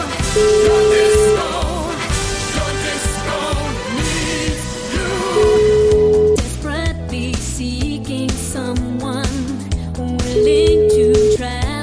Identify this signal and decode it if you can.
Ringback tone (Europe, 425 Hz), with noise.